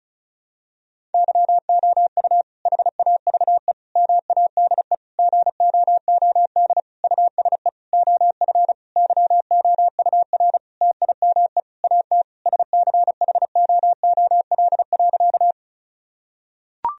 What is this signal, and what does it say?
Morse code, heard as a keyed tone, about 36 wpm, 700 Hz. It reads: YOU HAVE MADE GOOD USE OF YOUR TIME AT SCHOOL.
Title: Sentences from Top 1000 Words - Review Part 1 - 35wpm